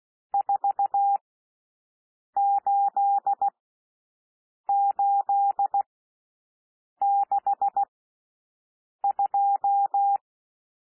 Morse code, heard as a keyed tone, about 16 wpm, spaced as for 6 wpm, 800 Hz. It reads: 48862
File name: cwmp3 / N1 SOUND NUMBERS 16 wpm